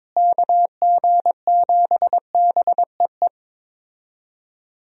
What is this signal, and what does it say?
Morse code, heard as a keyed tone, about 22 wpm, 700 Hz. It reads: KG7BEE
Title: USA Call Signs - 6 Characters - Repeated 3x - 22wpm